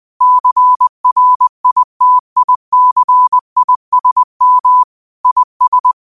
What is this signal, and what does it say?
Morse code, heard as a keyed tone, about 20 wpm, 1000 Hz. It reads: CRITICISM IS
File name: 20wpm